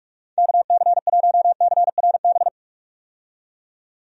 Morse code, heard as a keyed tone, about 46 wpm, 700 Hz. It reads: KC1CRB
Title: USA Call Signs - 6 Characters - 45wpm